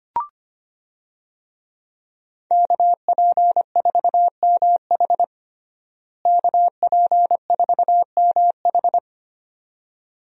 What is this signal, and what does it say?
Morse code, heard as a keyed tone, about 25 wpm, 700 Hz. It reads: KP4MH KP4MH
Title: DXCC-100 Call Signs - 5 Characters - Repeated 2x - 25wpm